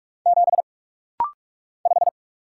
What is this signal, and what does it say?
Morse code, heard as a keyed tone, about 46 wpm, 700 Hz. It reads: 7 5
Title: Single Letter-Number - Rapid-Fire 45wpm